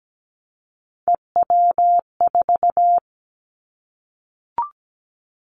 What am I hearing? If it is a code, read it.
Morse code, heard as a keyed tone, about 17 wpm, 700 Hz. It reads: EW4
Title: Callsign Prefixes for Top 100 DXCC - 17wpm